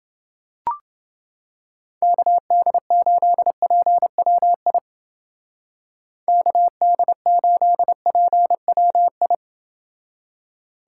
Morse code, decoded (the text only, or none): KD8PWI KD8PWI